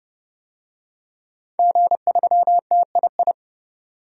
Morse code, heard as a keyed tone, about 30 wpm, 700 Hz. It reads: G3TII
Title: DXCC-100 Call Signs - 5 Characters - 30wpm